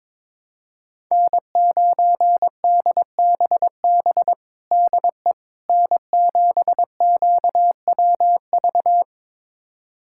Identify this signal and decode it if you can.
Morse code, heard as a keyed tone, about 22 wpm, 700 Hz. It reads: N9DBB DE N7QWV
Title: USA Callsign DE Callsign - 5 Characters - 22wpm